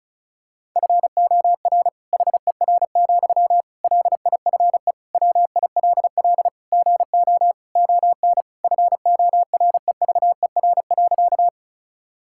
Morse code, heard as a keyed tone, about 36 wpm, 700 Hz. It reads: FOR HER, LIFE WILL GO ON FOREVER.
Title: Sentences from Top 1600 Words 35wpm